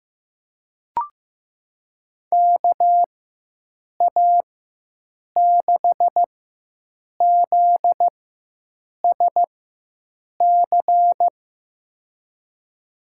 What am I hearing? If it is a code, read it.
Morse code, heard as a keyed tone, about 15 wpm, spaced as for 7 wpm, 700 Hz. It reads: KA6ZSC
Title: USA Call Signs - 6 Characters - 4x Character Spacing - 15wpm